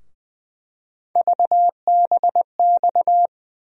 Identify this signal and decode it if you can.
Morse code, heard as a keyed tone, about 20 wpm, 700 Hz. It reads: VBX
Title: USA Call Sign Postfixes - 3 Letters 20wpm